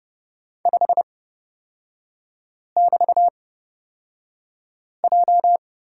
Morse code, heard as a keyed tone, about 30 wpm, spaced as for 5 wpm, 700 Hz. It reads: 5<BT>J
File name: Lesson 36e - BT - Three Characters 30wpm